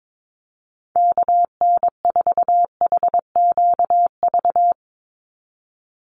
Morse code, heard as a keyed tone, about 22 wpm, 700 Hz. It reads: KN4HQV
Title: USA Call Signs 22wpm